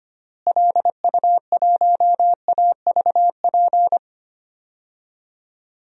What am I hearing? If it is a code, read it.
Morse code, heard as a keyed tone, about 25 wpm, 700 Hz. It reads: LU1AVP
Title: DXCC-100 Call Signs - 6 Characters - Repeated 3x - 25wpm